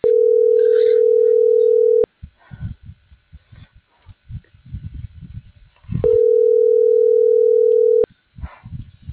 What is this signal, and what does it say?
Ringback tone (North America), 440 + 480 Hz.